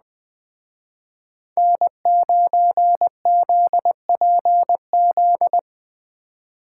Morse code, heard as a keyed tone, about 20 wpm, 700 Hz. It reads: N9ZPZ